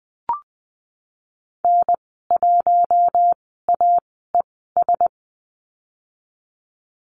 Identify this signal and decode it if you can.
Morse code, heard as a keyed tone, about 20 wpm, spaced as for 15 wpm, 700 Hz. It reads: N1AES
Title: 055 Lesson 10g - Letter r - Call Signs 20wpm